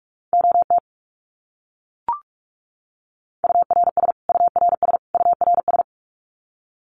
Morse code, heard as a keyed tone, about 44 wpm, 700 Hz. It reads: OT URS URS URS
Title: Abbreviations Repeated 3x - 45wpm